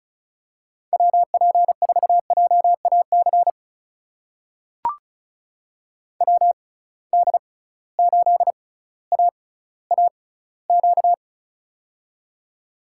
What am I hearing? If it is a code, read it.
Morse code, heard as a keyed tone, about 36 wpm, spaced as for 18 wpm, 700 Hz. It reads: WP4JAC WD8AAQ
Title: USA Call Signs - 6 Characters - 6x Character Spacing - 35wpm